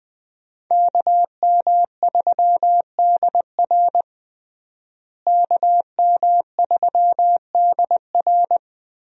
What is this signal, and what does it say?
Morse code, heard as a keyed tone, about 20 wpm, 700 Hz. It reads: KM3DR KM3DR
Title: USA Call Signs - 5 Characters - Repeated 3x - 20wpm